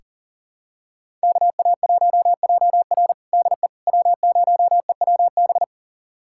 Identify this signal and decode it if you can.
Morse code, heard as a keyed tone, about 37 wpm, 700 Hz. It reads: KA1JR DE W0EWB
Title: USA Callsign DE Callsign - 5 Characters - 40wpm